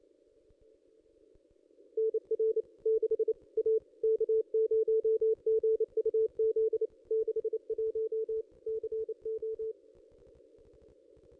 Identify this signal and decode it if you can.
Morse code, heard as a keyed tone, about 28 wpm, 440 Hz. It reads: NR 6 A K0GUZ 61 CO
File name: contests / ss-cw-k5tr-op-n5rz -1900